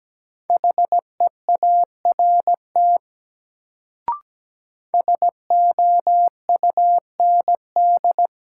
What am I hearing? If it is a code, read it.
Morse code, heard as a keyed tone, about 17 wpm, 700 Hz. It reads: HEART SOUND
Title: Sets of 3 Words - New Word Drill - 17wpm